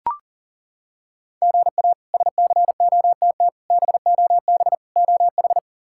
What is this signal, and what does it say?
Morse code, heard as a keyed tone, about 40 wpm, 700 Hz. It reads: GA SCOTT BOB OH
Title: K1USN SST QSO Practice - 40wpm